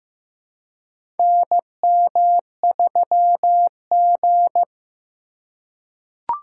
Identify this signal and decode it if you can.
Morse code, heard as a keyed tone, about 15 wpm, 700 Hz. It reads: NM3G
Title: USA Call Signs - 4 Characters - 2x Character Spacing - 15wpm